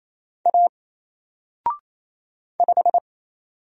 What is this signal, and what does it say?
Morse code, heard as a keyed tone, about 27 wpm, 700 Hz. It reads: A 5
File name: Single-Letter-Number-28wpm